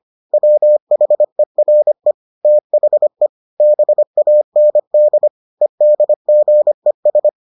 Morse code, heard as a keyed tone, about 25 wpm, 600 Hz. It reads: WHERE THE BAND EDGES